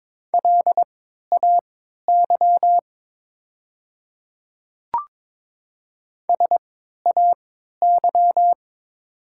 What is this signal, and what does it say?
Morse code, heard as a keyed tone, about 22 wpm, spaced as for 12 wpm, 700 Hz. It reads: LAY SAY
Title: 3x Character Spacing - Top 500 Words - 22wpm